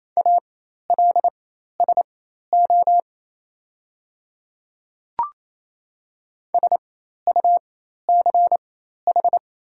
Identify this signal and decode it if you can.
Morse code, heard as a keyed tone, about 28 wpm, spaced as for 13 wpm, 690 Hz. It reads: ALSO SUCH
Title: 4x Character Spacing - Top 200 Words - 28wpm